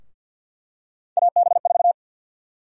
Morse code, heard as a keyed tone, about 52 wpm, 700 Hz. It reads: AB4
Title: USA Call Sign Prefixes - Repeated 2x - 50wpm